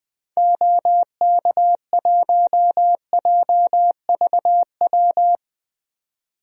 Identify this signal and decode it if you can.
Morse code, heard as a keyed tone, about 21 wpm, 700 Hz. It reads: OK1JVW
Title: DXCC-100 Call Signs - 6 Characters - 20wpm